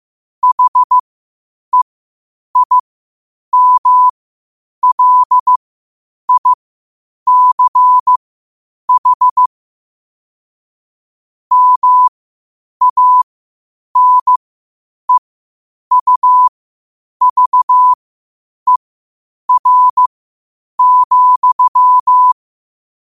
Morse code, heard as a keyed tone, about 15 wpm, spaced as for 8 wpm, 1000 Hz. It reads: HEIMLICH MANEUVER,
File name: qotdfiles / 8wpm